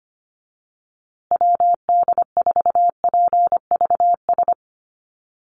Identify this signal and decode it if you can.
Morse code, heard as a keyed tone, about 25 wpm, 700 Hz. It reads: WD4PVS
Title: USA Call Signs Repeated 2x 25wpm